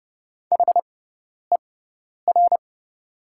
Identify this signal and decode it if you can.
Morse code, heard as a keyed tone, about 29 wpm, spaced as for 10 wpm, 690 Hz. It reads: HER
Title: 6x Character Spacing - Top 200 Words - 30wpm